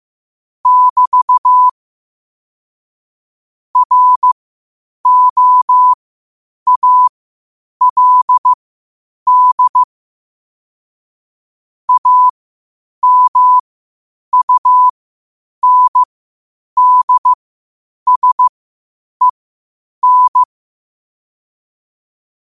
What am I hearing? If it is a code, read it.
Morse code, heard as a keyed tone, about 15 wpm, spaced as for 8 wpm, 1000 Hz. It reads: <BT> ROALD AMUNDSEN
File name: qotdfiles / 8wpm